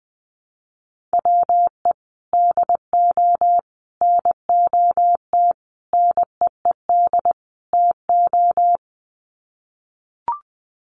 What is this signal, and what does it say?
Morse code, heard as a keyed tone, about 20 wpm, 700 Hz. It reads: WE DO NOT NEED TO
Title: Sets of 5 Words Encore 20wpm